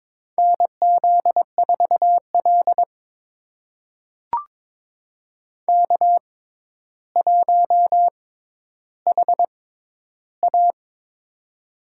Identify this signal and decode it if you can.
Morse code, heard as a keyed tone, about 22 wpm, spaced as for 11 wpm, 700 Hz. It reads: NZ4L K1HA
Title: USA Call Signs - 4 Characters - 6x Character Spacing - 22wpm